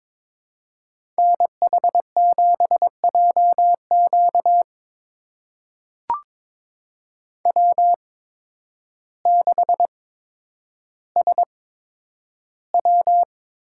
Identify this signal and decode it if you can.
Morse code, heard as a keyed tone, about 21 wpm, spaced as for 10 wpm, 700 Hz. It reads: NH7JQ W6SW
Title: USA Call Signs - 5 Characters - 8x Character Spacing - 22wpm